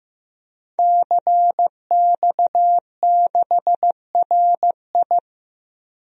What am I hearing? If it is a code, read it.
Morse code, heard as a keyed tone, about 15 wpm, 700 Hz. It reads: CX6RI
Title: DXCC-100 Call Signs - 5 Characters - 15wpm